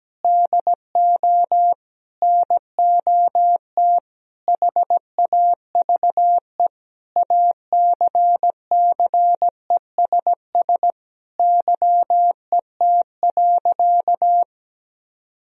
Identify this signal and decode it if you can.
Morse code, heard as a keyed tone, about 17 wpm, 700 Hz. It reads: DO NOT HAVE ACCESS YET.
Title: Sentences from Top 1000 Words - Review Part 4 - 17wpm